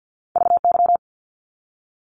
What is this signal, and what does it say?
Morse code, heard as a keyed tone, about 50 wpm, 700 Hz. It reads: VY
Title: Abbreviations 50wpm